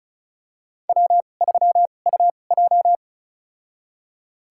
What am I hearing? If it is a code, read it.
Morse code, heard as a keyed tone, about 36 wpm, spaced as for 25 wpm, 700 Hz. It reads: W3UJ
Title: USA Call Signs - 4 Characters - 2x Character Spacing - 35wpm